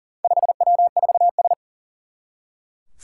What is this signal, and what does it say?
Morse code, heard as a keyed tone, about 40 wpm, 700 Hz. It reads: 5W4S
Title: Four Challenging Letter-Number Combinations 40wpm